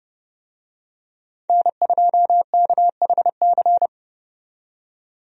Morse code, heard as a keyed tone, about 29 wpm, 700 Hz. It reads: N2KHC
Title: USA Call Signs - 5 Characters - 30wpm